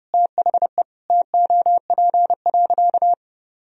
Morse code, heard as a keyed tone, about 30 wpm, 700 Hz. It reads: THE TOP.
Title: Sentences from Top 1000 Words - Review Part 2 - 30wpm